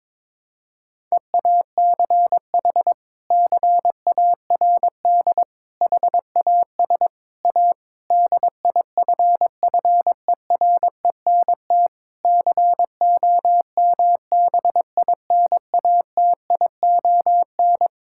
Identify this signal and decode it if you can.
Morse code, heard as a keyed tone, about 22 wpm, 700 Hz. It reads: EACH CARD HAS A DIFFERENT COMBINATION
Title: Sentences from Top 1000 Words - Review Part 6 - 22wpm